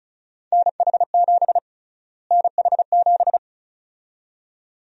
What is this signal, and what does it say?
Morse code, heard as a keyed tone, about 36 wpm, 700 Hz. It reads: NH7 NH7
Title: USA Call Sign Prefixes - Repeated 2x - 35wpm